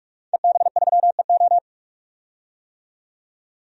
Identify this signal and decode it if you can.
Morse code, heard as a keyed tone, about 46 wpm, 700 Hz. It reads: EB3EO
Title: DXCC-100 Call Signs - 5 Characters - 45wpm